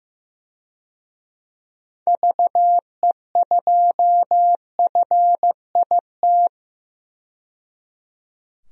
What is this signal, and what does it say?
Morse code, heard as a keyed tone, about 15 wpm, 700 Hz. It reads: VE2FIT
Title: DXCC-100 Call Signs - 6 Characters - 15wpm